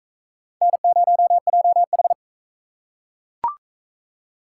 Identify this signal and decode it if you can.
Morse code, heard as a keyed tone, about 35 wpm, 700 Hz. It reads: N0JH
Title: Speed Racing - USA Call Signs 28wpm